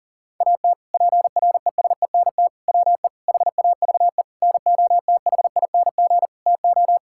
Morse code, heard as a keyed tone, about 40 wpm, 700 Hz. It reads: AT PRESENT WE HAVE NOTHING TO
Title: Sentences from Top 1000 Words - Review Part 5 - 40wpm